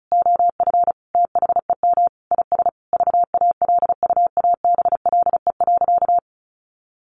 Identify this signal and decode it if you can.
Morse code, heard as a keyed tone, about 34 wpm, 700 Hz. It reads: OF THEM IS VALUABLE.
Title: Sentences from Top 1100 Words 35wpm